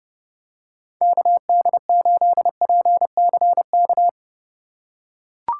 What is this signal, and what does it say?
Morse code, heard as a keyed tone, about 31 wpm, 700 Hz. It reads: KD8PCK